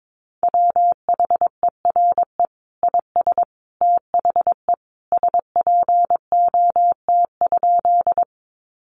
Morse code, heard as a keyed tone, about 22 wpm, 700 Hz. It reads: WHERE IS THE SPOT?
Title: Sentences from Top 1100 Words 22wpm